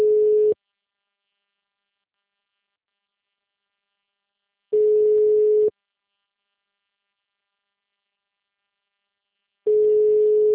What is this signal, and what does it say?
Ringback tone (Europe, 425 Hz).
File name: out- -2002- - - .69134